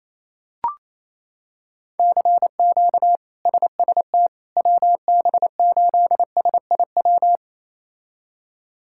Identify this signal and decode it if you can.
Morse code, heard as a keyed tone, about 28 wpm, 700 Hz. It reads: CQ SST WB8SIW
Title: K1USN SST QSO Practice - 28wpm